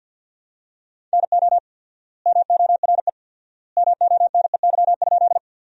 Morse code, heard as a keyed tone, about 52 wpm, 700 Hz. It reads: NO MORE MONEY?
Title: 4x Word-Spacing - Sentences from Top 400 Words - 50wpm